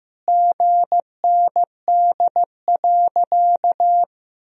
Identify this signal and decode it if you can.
Morse code, heard as a keyed tone, about 14 wpm, 700 Hz. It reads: GND.
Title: Sentences from Top 500 Words and Abbreviations 15wpm